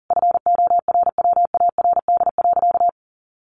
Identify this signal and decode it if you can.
Morse code, heard as a keyed tone, about 40 wpm, 700 Hz. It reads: FORWARD.
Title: Sentences from Top 1300 Words 40wpm